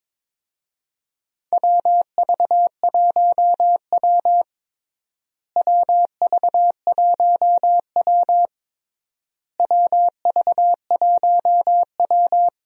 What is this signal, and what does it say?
Morse code, heard as a keyed tone, about 22 wpm, 700 Hz. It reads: WV1W WV1W WV1W
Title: USA Call Signs - 4 Characters - Repeated 3x - 22wpm